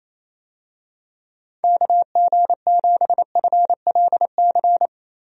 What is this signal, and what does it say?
Morse code, heard as a keyed tone, about 28 wpm, 700 Hz. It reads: KG7FLC
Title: USA Call Signs - 6 Characters - Repeated 2x - 28wpm